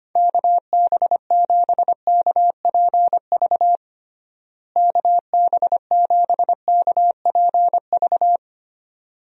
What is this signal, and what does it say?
Morse code, heard as a keyed tone, about 25 wpm, 700 Hz. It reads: KB7KPV KB7KPV
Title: USA Call Signs - 6 Characters - Repeated 3x - 25wpm